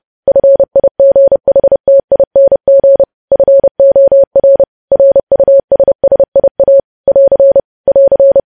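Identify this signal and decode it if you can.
Morse code, heard as a keyed tone, about 30 wpm, 550 Hz. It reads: FIGHTING FOR RUSSIA <AR> <AR>
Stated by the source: News headlines sent in Morse at 30 words per minute.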